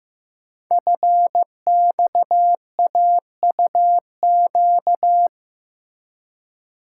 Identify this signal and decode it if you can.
Morse code, heard as a keyed tone, about 15 wpm, 700 Hz. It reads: FXAUQ
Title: Code Groups - 5 Letters - 15wpm